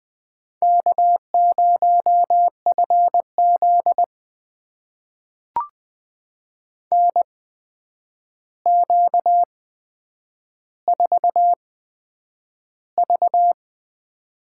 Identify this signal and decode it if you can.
Morse code, heard as a keyed tone, about 20 wpm, spaced as for 9 wpm, 700 Hz. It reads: K0FZ NQ4V
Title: USA Call Signs - 4 Characters - 8x Character Spacing - 20wpm